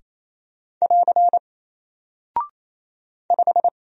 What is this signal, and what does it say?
Morse code, heard as a keyed tone, about 27 wpm, 700 Hz. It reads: <AR> 5
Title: Single Character (Letter:Number:Punctuation:Prosign) Alt 28wpm